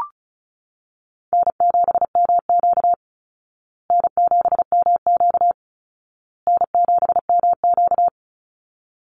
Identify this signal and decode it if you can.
Morse code, heard as a keyed tone, about 34 wpm, 700 Hz. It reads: N7MQ N7MQ N7MQ
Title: Easy Call Signs Repeated 3x - 35wpm